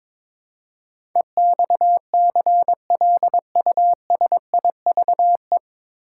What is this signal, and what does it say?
Morse code, heard as a keyed tone, about 22 wpm, 700 Hz. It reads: EXCLUSIVE
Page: Morse Code Ninja practice 22wpm